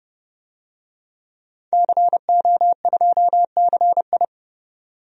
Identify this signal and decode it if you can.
Morse code, heard as a keyed tone, about 30 wpm, 700 Hz. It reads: CO2CI
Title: DXCC-100 Call Signs - 5 Characters - 30wpm